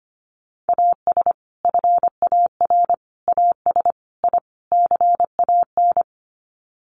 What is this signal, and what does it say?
Morse code, heard as a keyed tone, about 25 wpm, 700 Hz. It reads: AS FAR AS I CAN
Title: Sets of 5 Words Encore 25wpm